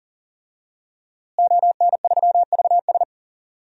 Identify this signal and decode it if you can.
Morse code, heard as a keyed tone, about 40 wpm, 700 Hz. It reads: ON3VS
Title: DXCC-100 Call Signs - 5 Characters - Repeated 3x - 40wpm